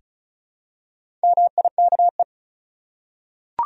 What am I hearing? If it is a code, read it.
Morse code, heard as a keyed tone, about 35 wpm, 700 Hz. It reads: MIKE
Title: USA Common Names - Max 4 Letters - 35wpm